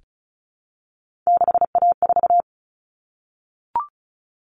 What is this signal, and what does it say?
Morse code, heard as a keyed tone, about 34 wpm, 700 Hz. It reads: 6A4
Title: Common Cut Numbers - 3-Digit 35wpm